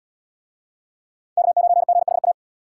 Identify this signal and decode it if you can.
Morse code, heard as a keyed tone, about 69 wpm, 700 Hz. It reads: K9GHA